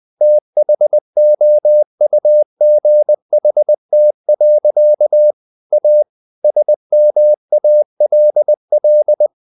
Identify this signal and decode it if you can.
Morse code, heard as a keyed tone, about 20 wpm, 600 Hz. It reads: THOUGHT. A SMALL
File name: the old man and the sea / the old man and the sea part 1 20wpm